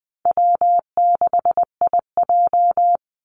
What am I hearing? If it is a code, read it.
Morse code, heard as a keyed tone, about 19 wpm, 700 Hz. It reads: W6IJ
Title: Easy Call Signs - 20wpm